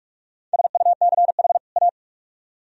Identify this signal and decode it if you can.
Morse code, heard as a keyed tone, about 57 wpm, 700 Hz. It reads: SUCH A